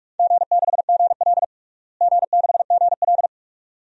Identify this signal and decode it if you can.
Morse code, heard as a keyed tone, about 46 wpm, 700 Hz. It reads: G6GL G6GL
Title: DXCC-100 Call Signs - 4 Characters - Repeated 2x - 45wpm